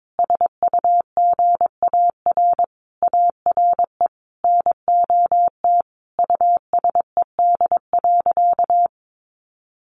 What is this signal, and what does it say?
Morse code, heard as a keyed tone, about 22 wpm, 700 Hz. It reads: SUGAR ARE NOT USED.
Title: Sentences from Top 1500 Words 22wpm